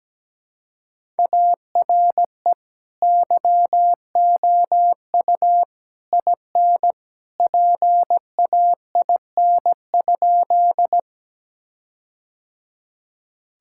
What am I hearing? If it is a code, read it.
Morse code, heard as a keyed tone, about 17 wpm, 700 Hz. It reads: ARE YOU IN PAIN?
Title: Sentences from Top 1600 Words 17wpm